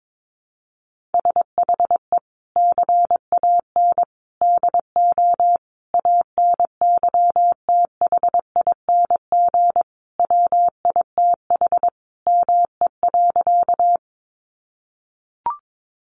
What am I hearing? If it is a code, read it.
Morse code, heard as a keyed tone, about 22 wpm, 700 Hz. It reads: SHE CAN DO ANYTHING WITH ME.
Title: Sentences from Top 1000 Words 22wpm